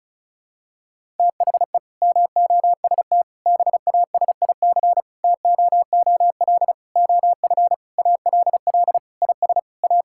Morse code, heard as a keyed tone, about 36 wpm, 700 Hz. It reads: THE MOST BASIC TOOL OF ALL IS A
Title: Sentences from Top 1000 Words - Review Part 6 - 35wpm